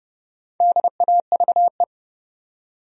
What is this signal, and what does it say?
Morse code, heard as a keyed tone, about 30 wpm, 700 Hz. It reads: DAVE
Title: USA Common Names - Max 4 Letters - 30wpm